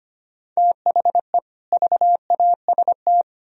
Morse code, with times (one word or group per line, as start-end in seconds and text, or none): 0.6-1.4 THE
1.7-3.2 VAST